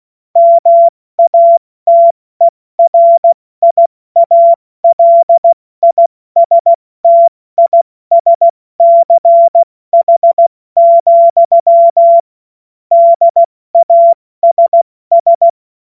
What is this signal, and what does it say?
Morse code, heard as a keyed tone, about 16 wpm, 680 Hz. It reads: MATERIALISTISCH, DASS
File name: Morsekurs / Witze / Witz 02. August 2022 70 F80